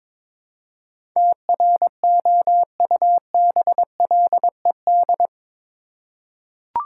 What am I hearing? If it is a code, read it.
Morse code, heard as a keyed tone, about 21 wpm, 700 Hz. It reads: TROUBLED
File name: practice / 22wpm / Eight Letter Words - 22wpm